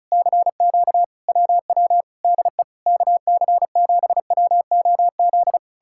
Morse code, heard as a keyed tone, about 40 wpm, 700 Hz. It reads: CQ WW DE KC7WOZ